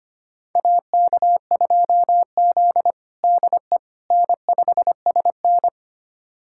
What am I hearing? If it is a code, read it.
Morse code, heard as a keyed tone, about 25 wpm, 700 Hz. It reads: AK2Z DE N5SN
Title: USA Callsign DE Callsign - 4 Characters - 25wpm